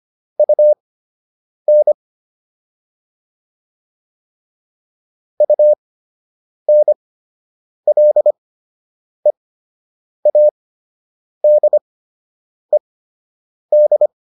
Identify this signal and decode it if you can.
Morse code, heard as a keyed tone, about 25 wpm, spaced as for 8 wpm, 600 Hz. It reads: UN UNLEADED